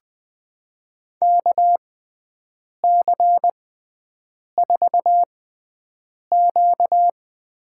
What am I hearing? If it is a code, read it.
Morse code, heard as a keyed tone, about 20 wpm, spaced as for 7 wpm, 700 Hz. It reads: KC4Q
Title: USA Call Signs - 6 Characters - 6x Character Spacing - 20wpm